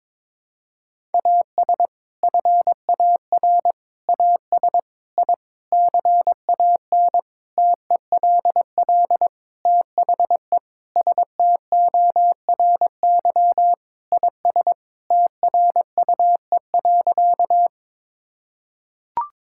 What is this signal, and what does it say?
Morse code, heard as a keyed tone, about 22 wpm, 700 Hz. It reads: AS FAR AS I CAN TELL THE STORY IS TRUE.
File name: Sentences with 7-Word Phrases - 22wpm